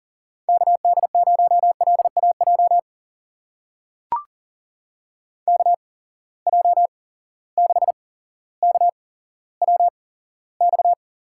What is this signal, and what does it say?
Morse code, heard as a keyed tone, about 38 wpm, spaced as for 17 wpm, 700 Hz. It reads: KD0LAJ KJ6KWX
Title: USA Call Signs - 6 Characters - 8x Character Spacing - 40wpm